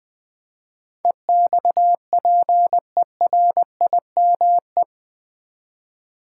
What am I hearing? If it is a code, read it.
Morse code, heard as a keyed tone, about 20 wpm, 700 Hz. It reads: EXPERIME
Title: Build A Word - 4 Letters At A Time - 12-Letter Words - 20wpm